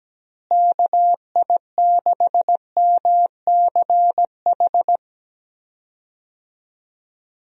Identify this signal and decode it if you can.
Morse code, heard as a keyed tone, about 17 wpm, 700 Hz. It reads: KI6MCH
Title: USA Call Signs - 6 Characters - Repeated 2x - 17wpm